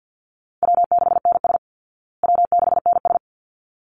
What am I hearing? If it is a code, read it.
Morse code, heard as a keyed tone, about 49 wpm, 700 Hz. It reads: W6NS W6NS
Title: Call Signs Repeated 3x - 50wpm